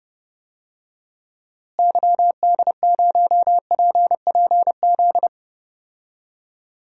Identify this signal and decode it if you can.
Morse code, heard as a keyed tone, about 30 wpm, 700 Hz. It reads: YD0PPZ
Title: DXCC-100 Call Signs - 6 Characters - Repeated 3x - 30wpm